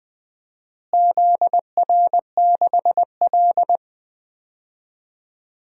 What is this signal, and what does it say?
Morse code, heard as a keyed tone, about 21 wpm, 700 Hz. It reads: ZR6L
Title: DXCC-100 Call Signs - 4 Characters - Repeated 2x - 20wpm